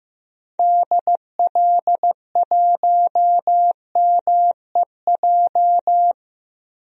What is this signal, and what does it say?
Morse code, heard as a keyed tone, about 15 wpm, 700 Hz. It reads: DL1MEJ
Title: DXCC-100 Call Signs - 6 Characters - 15wpm